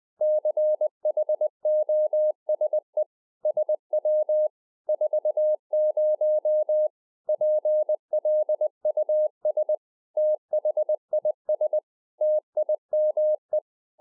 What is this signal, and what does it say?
Morse code, heard as a keyed tone, about 20 wpm, 610 Hz. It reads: CHOSE SW 40 PLUS THIS TIME
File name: 2011-04-mp3 / A SMALL TRANCEIVER 3 20WPM